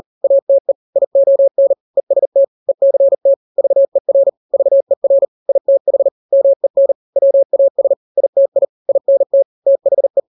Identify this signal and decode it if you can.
Morse code, heard as a keyed tone, about 40 wpm, 550 Hz. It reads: ATE ION EST ECT VER VER ITH MEN WAS ITI INT THE
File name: n-grams / 3-grams 40 35 WPM